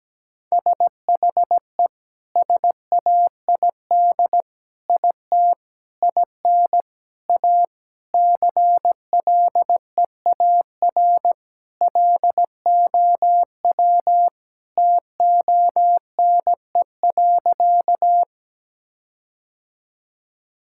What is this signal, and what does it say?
Morse code, heard as a keyed tone, about 17 wpm, 700 Hz. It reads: SHE SAID IT IN A CLEAR LOW TONE.
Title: Sentences from Top 1600 Words 17wpm